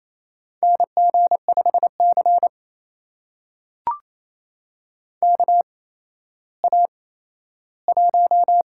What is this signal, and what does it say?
Morse code, heard as a keyed tone, about 28 wpm, spaced as for 14 wpm, 700 Hz. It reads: NG5C KA1
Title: USA Call Signs - 8x Character Spacing - 28wpm